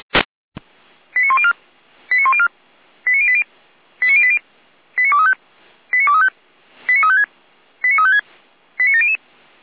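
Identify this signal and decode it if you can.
Five-tone selective call (ZVEI), heard as quick runs of five tones, tones of about 70 ms. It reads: 89104 89104 89080 89080 89235 89235 89356 89356 89700
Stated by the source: Five-tone ZVEI selective call received on an SDR, address 89092, channel 455.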